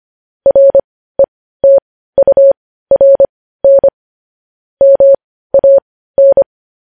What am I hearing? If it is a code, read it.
Morse code, heard as a keyed tone, about 25 wpm, spaced as for 15 wpm, 550 Hz. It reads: RETURN MAN
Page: news headlines in morse code at 15 wpm